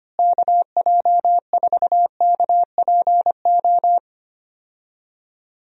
Morse code, heard as a keyed tone, about 25 wpm, 700 Hz. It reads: KJ4KPO
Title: USA Call Signs - 6 Characters - Repeated 2x - 25wpm